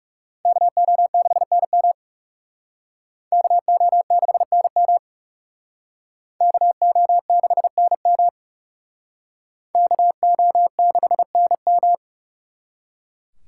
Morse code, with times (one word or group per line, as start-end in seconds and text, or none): 0.5-1.9 KO6NM
3.3-5.0 KO6NM
6.4-8.3 KO6NM
9.8-11.9 KO6NM